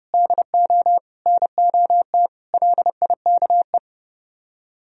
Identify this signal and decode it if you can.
Morse code, heard as a keyed tone, about 38 wpm, 700 Hz. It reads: DO NOT LIKE